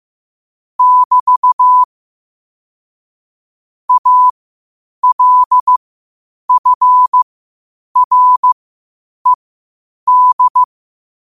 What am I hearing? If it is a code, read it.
Morse code, heard as a keyed tone, about 15 wpm, spaced as for 8 wpm, 1000 Hz. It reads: <BT> ALFRED